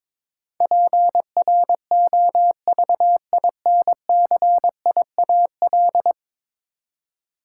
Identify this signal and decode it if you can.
Morse code, heard as a keyed tone, about 22 wpm, 700 Hz. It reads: PROVINCIAL
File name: practice / 22wpm / 10 Letter Words - First 3 Letters Given - 22wpm